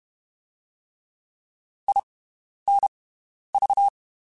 Morse code, heard as a keyed tone, about 32 wpm, 800 Hz. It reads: I N V
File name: (spaced) / QSOs Spaced-mp3s / WD0FBT 1 @30 sp